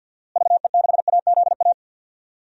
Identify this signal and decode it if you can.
Morse code, heard as a keyed tone, about 52 wpm, 700 Hz. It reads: VE6AZA
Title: DXCC-100 Call Signs - 6 Characters - Repeated 3x - 50wpm